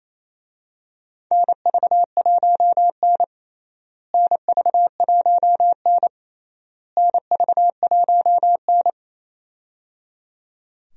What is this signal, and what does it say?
Morse code, heard as a keyed tone, about 28 wpm, 700 Hz. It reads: NV1N NV1N NV1N
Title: USA Call Signs - 4 Characters - Repeated 3x - 28wpm